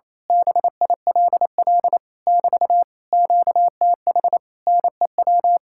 Morse code, heard as a keyed tone, about 26 wpm, 700 Hz. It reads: BILL <BT> QTH NEW
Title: Speed-Racing Common QSO Phrases Encore - 28wpm